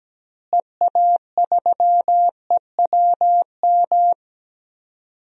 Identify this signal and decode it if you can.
Morse code, heard as a keyed tone, about 17 wpm, 700 Hz. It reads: EA3EWM
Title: DXCC-100 Call Signs - 6 Characters - Repeated 3x - 17wpm